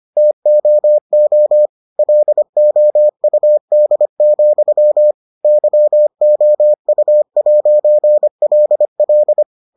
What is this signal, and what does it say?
Morse code, heard as a keyed tone, about 25 wpm, 600 Hz. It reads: TOO LOUD, YOU'LL